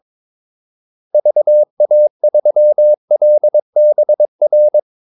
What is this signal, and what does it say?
Morse code, heard as a keyed tone, about 22 wpm, 600 Hz. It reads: VA3LBR
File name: Callsigns-22wpm